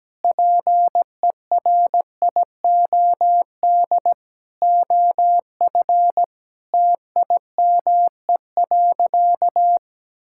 Morse code, heard as a keyed tone, about 17 wpm, 700 Hz. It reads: PERIOD OF TIME.
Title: Sentences from Top 1000 Words - Review Part 5 - 17wpm